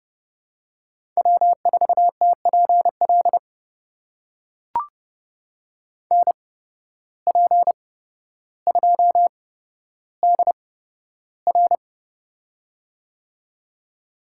Morse code, heard as a keyed tone, about 32 wpm, spaced as for 13 wpm, 700 Hz. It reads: W4TPL NP2DR
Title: USA Call Signs - 5 Characters - 8x Character Spacing - 30wpm